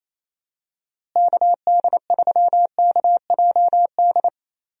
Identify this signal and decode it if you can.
Morse code, heard as a keyed tone, about 28 wpm, 700 Hz. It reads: KD3KJD
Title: USA Call Signs - 6 Characters - Repeated 2x - 28wpm